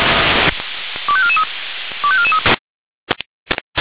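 Five-tone selective call (ZVEI), heard as quick runs of five tones, tones of about 70 ms. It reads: 25443 25443
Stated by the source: Received with an SDR; ZVEI five-tone call 25441.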